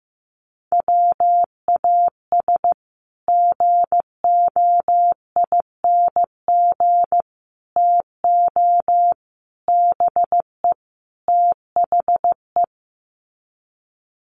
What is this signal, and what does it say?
Morse code, heard as a keyed tone, about 15 wpm, 700 Hz. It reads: WAS GOING TO BE THE
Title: Sets of 5 Words Encore2 15wpm